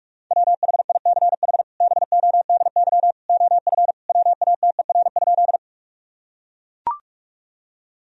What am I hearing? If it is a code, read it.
Morse code, heard as a keyed tone, about 41 wpm, 700 Hz. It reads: WHICH BODY OF WATER?
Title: Sentences from Top 1000 Words - Review Part 2 - 45wpm